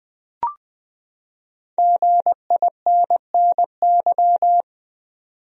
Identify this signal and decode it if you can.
Morse code, heard as a keyed tone, about 20 wpm, 700 Hz. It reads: GINNY
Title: USA Common Names - 20wpm